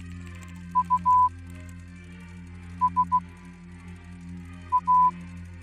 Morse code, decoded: U S A